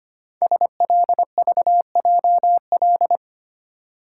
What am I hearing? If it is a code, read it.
Morse code, heard as a keyed tone, about 25 wpm, 700 Hz. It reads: SLVJL
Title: Code Groups - 5 Letters - 25wpm